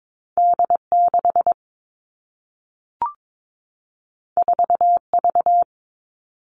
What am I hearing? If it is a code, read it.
Morse code, heard as a keyed tone, about 22 wpm, 700 Hz. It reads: D 6 4 V
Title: Two Challenging Letter-Number Combinations 22wpm